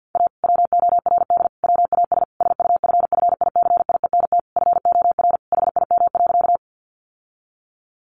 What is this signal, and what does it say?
Morse code, heard as a keyed tone, about 50 wpm, 700 Hz. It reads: A WORD WAS SUFFICIENT FOR HIM.
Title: Sentences from Top 1400 Words 50wpm